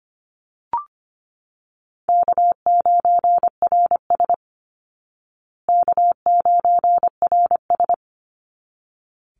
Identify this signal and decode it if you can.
Morse code, heard as a keyed tone, about 25 wpm, 700 Hz. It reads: K9RS K9RS
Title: Easy Call Signs Repeated 2x - 25wpm